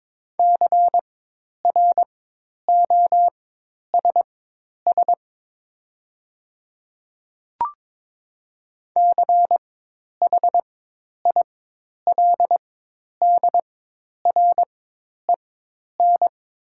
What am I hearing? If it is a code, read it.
Morse code, heard as a keyed tone, about 22 wpm, spaced as for 10 wpm, 700 Hz. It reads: CROSS CHILDREN
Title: 4x Character Spacing - Top 500 Words - 22wpm